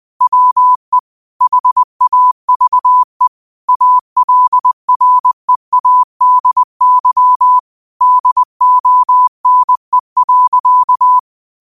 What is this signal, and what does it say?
Morse code, heard as a keyed tone, about 20 wpm, 1000 Hz. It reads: WE HAVE ALREADY DONE.